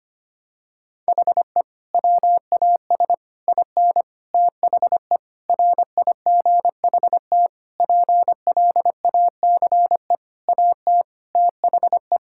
Morse code, heard as a keyed tone, about 25 wpm, 700 Hz. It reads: HE WAS IN THE RIGHT PLACE AT THE
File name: practice / 25wpm / Sentences with 9-Word Phrases - 25wpm